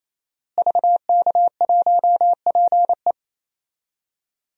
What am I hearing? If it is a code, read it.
Morse code, heard as a keyed tone, about 27 wpm, 700 Hz. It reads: VK1PE
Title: DXCC-100 Call Signs - 5 Characters - Repeated 3x - 28wpm